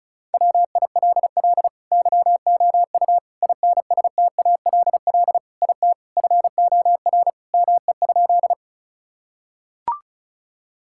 Morse code, heard as a keyed tone, about 36 wpm, 700 Hz. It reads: WILL YOU INSTALL IT FOR ME?
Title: Sentences from Top 1600 Words 35wpm